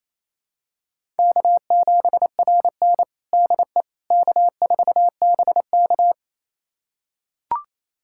Morse code, heard as a keyed tone, about 28 wpm, 700 Hz. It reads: K7RN DE K4BK
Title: USA Callsign DE Callsign - 4 Characters - 28wpm